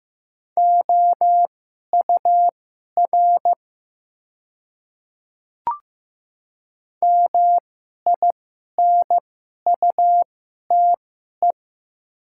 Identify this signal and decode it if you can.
Morse code, heard as a keyed tone, about 15 wpm, spaced as for 11 wpm, 700 Hz. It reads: OUR MINUTE
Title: 2x Character Spacing - Top 500 Words - 15wpm